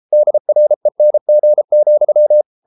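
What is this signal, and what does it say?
Morse code, heard as a keyed tone, about 33 wpm, 600 Hz. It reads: DRENG,
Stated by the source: Morse code at 33 wpm, QRQ.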